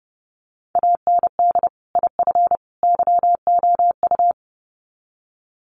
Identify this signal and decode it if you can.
Morse code, heard as a keyed tone, about 30 wpm, 700 Hz. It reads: AND IF YOU
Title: Sets of 3 Words 30wpm